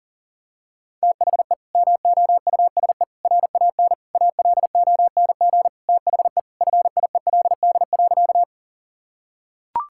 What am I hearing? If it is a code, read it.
Morse code, heard as a keyed tone, about 40 wpm, 700 Hz. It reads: THE MOUSE RAN ALONG THE FIELD.
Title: Sentences from Top 1000 Words - Review Part 5 - 40wpm